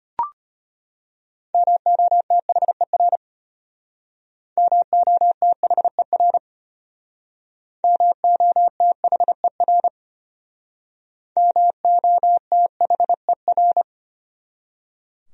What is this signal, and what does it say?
Morse code, heard as a keyed tone, about 32 wpm, 700 Hz. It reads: MOTHER MOTHER MOTHER MOTHER